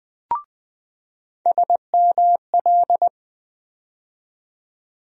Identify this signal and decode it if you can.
Morse code, heard as a keyed tone, about 20 wpm, 700 Hz. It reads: SML
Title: USA Call Sign Postfixes - 3 Letters 20wpm